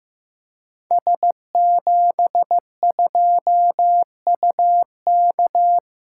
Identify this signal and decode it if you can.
Morse code, heard as a keyed tone, about 15 wpm, 700 Hz. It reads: S72UK
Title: Code Groups - 5 Letter:Numbers - 15wpm